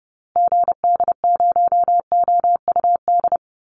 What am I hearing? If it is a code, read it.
Morse code, heard as a keyed tone, about 30 wpm, 700 Hz. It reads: GD0OUD
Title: Call Signs Repeated 2x - 30wpm